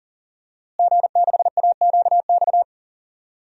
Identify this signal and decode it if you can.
Morse code, heard as a keyed tone, about 40 wpm, 700 Hz. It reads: G6AQX
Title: DXCC-100 Call Signs - 5 Characters - Repeated 3x - 40wpm